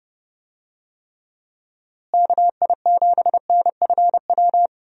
Morse code, heard as a keyed tone, about 30 wpm, 700 Hz. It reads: KI7NFW
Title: USA Call Signs - 6 Characters - Repeated 2x - 30wpm